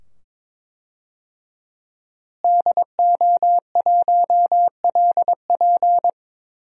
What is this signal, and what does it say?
Morse code, heard as a keyed tone, about 22 wpm, 700 Hz. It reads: DO1LP
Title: DXCC-100 Call Signs - 5 Characters - Repeated 2x - 22wpm